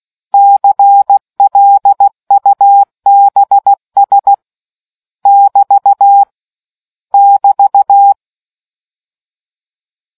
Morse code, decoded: CLUBS <BT> <BT>

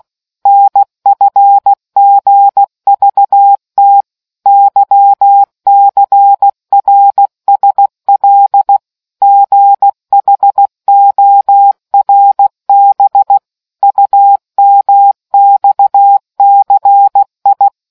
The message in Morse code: NFGVT YCRSL GHORB UMXCI

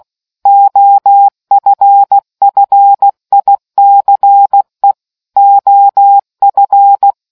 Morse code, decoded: OFFICE OF

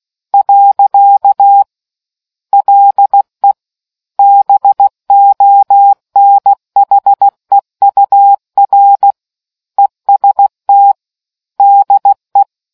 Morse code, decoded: . LE BONHEUR EST DE

